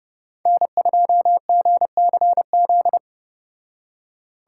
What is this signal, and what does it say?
Morse code, heard as a keyed tone, about 29 wpm, 700 Hz. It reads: N2GCZ